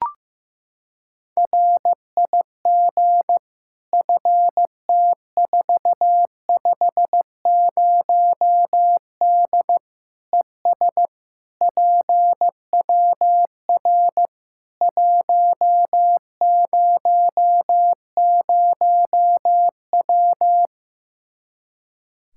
Morse code, heard as a keyed tone, about 15 wpm, 700 Hz. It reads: RIG FT450D ES PWR 100W
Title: Practice QSO - Exchange 2 - Info - 15wpm